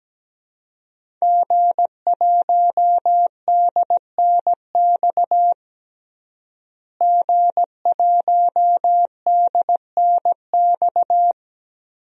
Morse code, heard as a keyed tone, about 17 wpm, 700 Hz. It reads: G1DNX G1DNX
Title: DXCC-100 Call Signs - 5 Characters - Repeated 3x - 17wpm